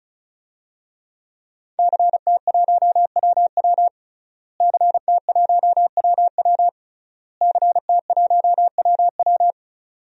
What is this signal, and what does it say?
Morse code, heard as a keyed tone, about 34 wpm, 700 Hz. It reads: CT1WW CT1WW CT1WW
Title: DXCC-100 Call Signs - 5 Characters - Repeated 3x - 35wpm